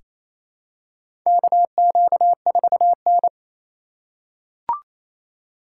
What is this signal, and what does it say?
Morse code, heard as a keyed tone, about 28 wpm, 700 Hz. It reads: KQ4N
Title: USA Call Signs - 4 Characters - 4x Character Spacing - 28wpm